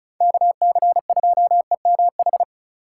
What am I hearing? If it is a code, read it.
Morse code, heard as a keyed tone, about 36 wpm, 700 Hz. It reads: KC2EMH
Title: USA Call Signs - 6 Characters - 2x Character Spacing - 35wpm